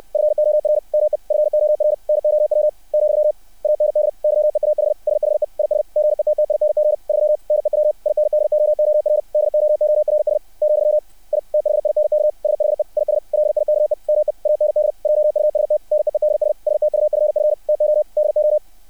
Morse code, heard as a keyed tone, about 60 wpm, 600 Hz. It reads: FOR ME FOR NOW , AND YEAH SHE IS GETTING 3 NEW INDOOR DOORS , I THINK SHE IS LIKE ME AND JUST NEEDS STUFF TO DO
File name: archived / CRSnet